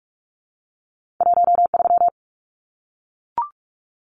Morse code, heard as a keyed tone, about 44 wpm, 700 Hz. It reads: 13
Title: Numbers - 2-Digit 45wpm